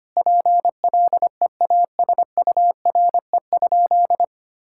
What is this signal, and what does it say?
Morse code, heard as a keyed tone, about 25 wpm, 700 Hz. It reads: PLEASURE?
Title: Sentences from Top 1600 Words 25wpm